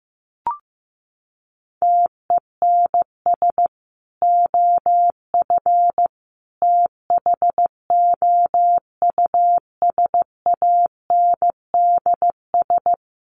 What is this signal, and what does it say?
Morse code, heard as a keyed tone, about 15 wpm, 700 Hz. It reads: TENS OF THOUSANDS
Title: Sets of 5 Words Encore2 15wpm